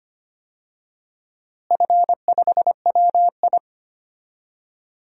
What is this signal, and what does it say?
Morse code, heard as a keyed tone, about 25 wpm, 700 Hz. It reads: F5WI
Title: DXCC-100 Call Signs - 4 Characters - Repeated 3x - 25wpm